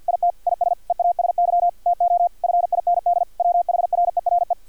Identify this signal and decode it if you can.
Morse code, heard as a keyed tone, about 50 wpm, 700 Hz, with radio noise. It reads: IT IS EASY TO FIND WHRERE
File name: archived / CRSnet